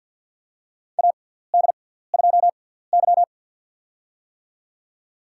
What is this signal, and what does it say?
Morse code, heard as a keyed tone, about 48 wpm, spaced as for 17 wpm, 700 Hz. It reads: AD2Y
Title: USA Call Signs - 4 Characters - 6x Character Spacing - 50wpm